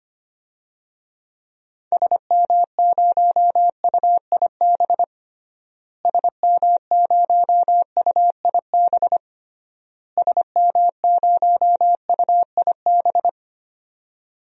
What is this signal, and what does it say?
Morse code, heard as a keyed tone, about 25 wpm, 700 Hz. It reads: SM0UIB SM0UIB SM0UIB
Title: DXCC-100 Call Signs - 6 Characters - Repeated 3x - 25wpm